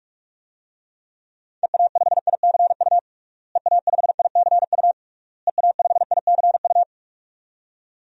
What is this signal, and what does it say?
Morse code, heard as a keyed tone, about 46 wpm, 700 Hz. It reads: EA5ICU EA5ICU EA5ICU
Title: DXCC-100 Call Signs - 6 Characters - Repeated 3x - 45wpm